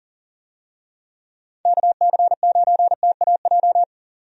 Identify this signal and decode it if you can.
Morse code, heard as a keyed tone, about 40 wpm, 700 Hz. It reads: KC9TAJ
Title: USA Call Signs - 6 Characters - 40wpm